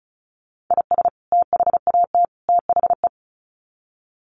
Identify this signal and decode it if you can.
Morse code, heard as a keyed tone, about 34 wpm, 700 Hz. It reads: IS THAT THE
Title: Sets of 3 Words 35wpm